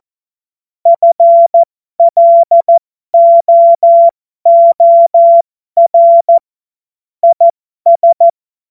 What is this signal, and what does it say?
Morse code, heard as a keyed tone, about 14 wpm, spaced as for 12 wpm, 680 Hz. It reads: FLOOR IS
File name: Morsekurs / Witze